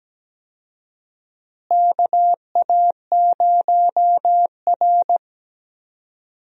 Morse code, heard as a keyed tone, about 17 wpm, 700 Hz. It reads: KA0R
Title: USA Call Signs - 4 Characters - Repeated 3x - 17wpm